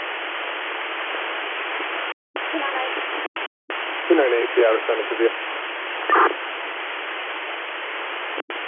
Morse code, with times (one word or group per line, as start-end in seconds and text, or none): none